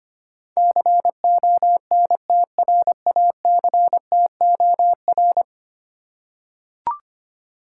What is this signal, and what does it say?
Morse code, heard as a keyed tone, about 25 wpm, 700 Hz. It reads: CONTRACTOR
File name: 10 Letter Words - First 3 Letters Given - 25wpm